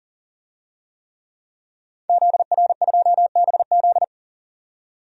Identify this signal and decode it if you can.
Morse code, heard as a keyed tone, about 40 wpm, 700 Hz. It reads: ZR2BZ
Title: DXCC-100 Call Signs - 5 Characters - 40wpm